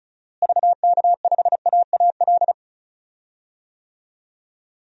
Morse code, heard as a keyed tone, about 34 wpm, 700 Hz. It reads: VK5AAL
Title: DXCC-100 Call Signs - 6 Characters - Repeated 2x - 35wpm